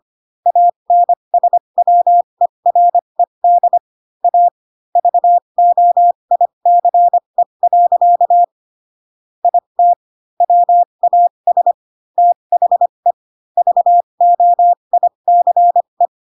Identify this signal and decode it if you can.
Morse code, heard as a keyed tone, about 25 wpm, spaced as for 22 wpm, 700 Hz. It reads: ANSWERED A VOICE. IT WAS THE VOICE
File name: Pinocchio / Pinocchio ch6 18 25 ik1hge